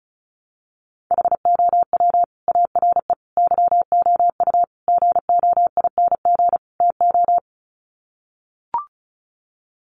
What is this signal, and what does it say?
Morse code, heard as a keyed tone, about 34 wpm, 700 Hz. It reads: HOW ARE YOU GOING TO
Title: Sets of 5 Words 35wpm